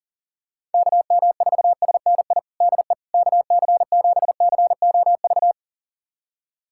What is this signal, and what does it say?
Morse code, heard as a keyed tone, about 40 wpm, 700 Hz. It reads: KM4SNI DE KC7COV
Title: USA Callsign DE Callsign 40wpm